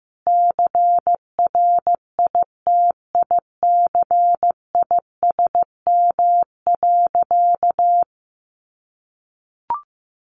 Morse code, heard as a keyed tone, about 15 wpm, 700 Hz. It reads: CRITICISM.
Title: Sentences from Top 1500 Words 15wpm